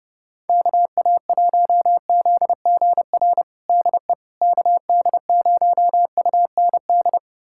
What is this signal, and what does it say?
Morse code, heard as a keyed tone, about 30 wpm, 700 Hz. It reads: KA1ZGR DE KD0UND
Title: USA Callsign DE Callsign - 6 Characters - 30wpm